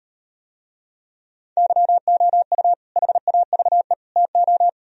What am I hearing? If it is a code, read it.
Morse code, heard as a keyed tone, about 38 wpm, 700 Hz. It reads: YOU HAVE TO